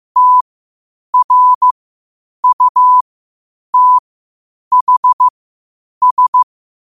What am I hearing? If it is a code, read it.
Morse code, heard as a keyed tone, about 15 wpm, spaced as for 8 wpm, 1000 Hz. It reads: TRUTHS
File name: 8wpm